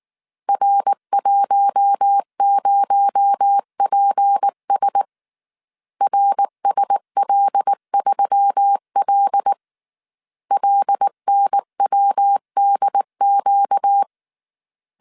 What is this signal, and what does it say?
Morse code, heard as a keyed tone, about 19 wpm, 800 Hz. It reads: R10PS RSL3L LNWDQ
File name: CW / DET / 1 6 1500C S A 18wpm 2255